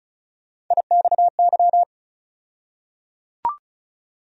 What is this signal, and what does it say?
Morse code, heard as a keyed tone, about 36 wpm, 700 Hz. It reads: IXY
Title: USA Call Sign Postfixes - 3 Letters 35wpm